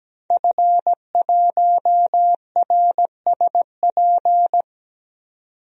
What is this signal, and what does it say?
Morse code, heard as a keyed tone, about 17 wpm, 700 Hz. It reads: F1RSP
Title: DXCC-100 Call Signs - 5 Characters - 17wpm